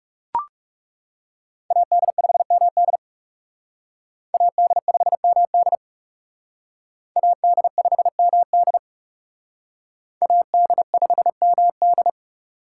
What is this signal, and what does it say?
Morse code, heard as a keyed tone, about 37 wpm, 700 Hz. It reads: AD5MD AD5MD AD5MD AD5MD
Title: Speed Racing - USA 2x2 Call Signs 30wpm